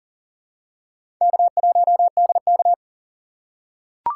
Morse code, heard as a keyed tone, about 40 wpm, 700 Hz. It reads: K1DK
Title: USA Call Signs - 2x Character Spacing - 40wpm